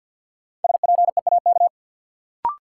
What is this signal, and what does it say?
Morse code, heard as a keyed tone, about 53 wpm, 700 Hz. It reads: SPEAK
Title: Sentences from Top 1100 Words - New Word Drill Alt - 50wpm